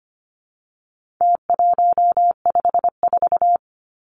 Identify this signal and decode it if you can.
Morse code, heard as a keyed tone, about 25 wpm, 700 Hz. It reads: T154
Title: Numbers - Contest Style - 4-Digit 25wpm